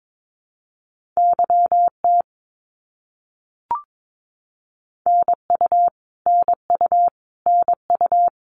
Morse code, heard as a keyed tone, about 22 wpm, 700 Hz. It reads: YT NU NU NU
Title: Canadian Provinces and Territories - Repeated 3x - 22wpm